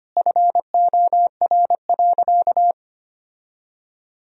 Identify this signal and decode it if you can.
Morse code, heard as a keyed tone, about 25 wpm, 700 Hz. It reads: FOR.
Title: Sentences from Top 1000 Words - Review Part 3 - 25wpm